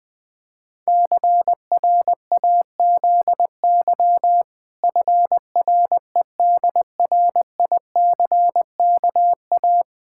Morse code, heard as a keyed tone, about 20 wpm, 700 Hz. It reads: CRAZY FREDRICKA
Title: Pangram Challenge v2 - 20wpm